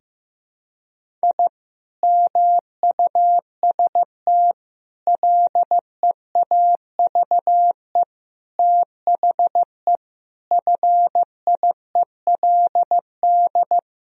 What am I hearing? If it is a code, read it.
Morse code, heard as a keyed tone, about 15 wpm, 700 Hz. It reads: I MUST LEAVE THE FIELD
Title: Sentences from Top 1000 Words - Review Part 2 - 15wpm